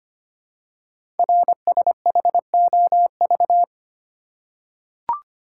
Morse code, heard as a keyed tone, about 25 wpm, 700 Hz. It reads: RSHOV
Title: Code Groups - 5 Letter:Numbers - 25wpm